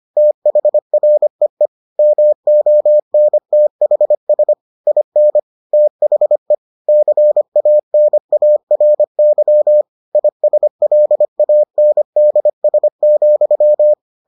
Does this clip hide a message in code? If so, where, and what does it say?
Yes: Morse code from 0.2 to 13.9 s, heard as a keyed tone, about 25 wpm, 600 Hz. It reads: THREE MONTHS IN THE CANARY ISLANDS,